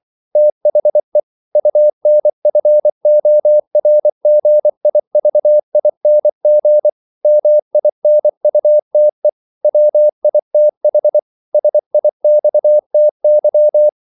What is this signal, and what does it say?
Morse code, heard as a keyed tone, about 24 wpm, 600 Hz. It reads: THE UNFORGIVING MINUTE WITH SIXTY
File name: CW / Practice / If24